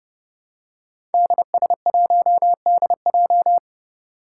Morse code, heard as a keyed tone, about 30 wpm, 700 Hz. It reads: DS1DJ
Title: Five Challenging Letter-Number Combinations 30wpm